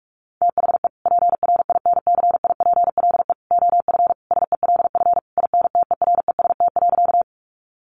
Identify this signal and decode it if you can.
Morse code, heard as a keyed tone, about 44 wpm, 700 Hz. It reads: THE PRINCIPLE OF SELF INTEREST.
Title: Sentences from Top 1400 Words 45wpm